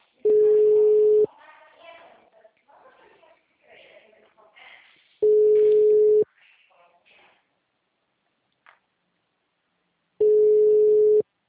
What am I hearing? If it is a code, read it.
Ringback tone (Europe, 425 Hz), with speech.
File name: out- -2051- - - .60367